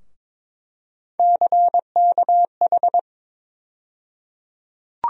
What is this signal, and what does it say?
Morse code, heard as a keyed tone, about 22 wpm, 700 Hz. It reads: CKH
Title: USA Call Sign Postfixes - 3 Letters 22wpm